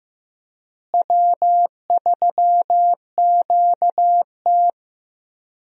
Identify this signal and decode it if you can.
Morse code, heard as a keyed tone, about 16 wpm, 700 Hz. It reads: W3QT
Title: USA Call Signs - 4 Characters - Repeated 2x - 15wpm